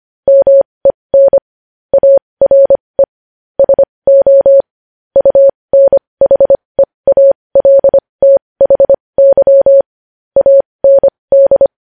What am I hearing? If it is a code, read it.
Morse code, heard as a keyed tone, about 25 wpm, spaced as for 20 wpm, 550 Hz. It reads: MEN ARE SO UNHEALTHY AND
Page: news headlines in morse code at 20 wpm